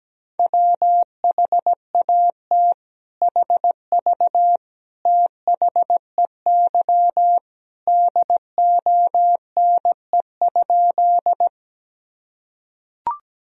Morse code, heard as a keyed tone, about 17 wpm, 700 Hz. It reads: WHAT HV THEY DONE?
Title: Sentences from Top 500 Words and Abbreviations 17wpm